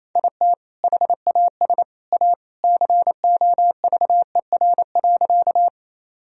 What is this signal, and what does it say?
Morse code, heard as a keyed tone, about 28 wpm, 700 Hz. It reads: IT HAS A COVER.
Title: Sentences from Top 1000 Words - Review Part 1 - 28wpm